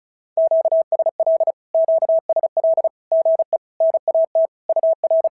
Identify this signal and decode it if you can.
Morse code, heard as a keyed tone, about 33 wpm, 650 Hz. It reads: QSL QSL GE NAT UR
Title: SOTA QSO Practice 35wpm